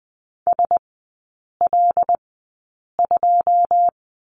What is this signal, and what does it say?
Morse code, heard as a keyed tone, about 20 wpm, spaced as for 8 wpm, 690 Hz. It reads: SL2
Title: 081 Lesson 14e - Number 2 - Three Characters 20wpm